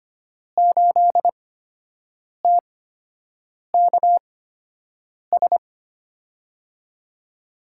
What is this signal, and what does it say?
Morse code, heard as a keyed tone, about 25 wpm, spaced as for 7 wpm, 700 Hz. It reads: 8TKS
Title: USA Call Signs - 5 Characters - 8x Character Spacing - 25wpm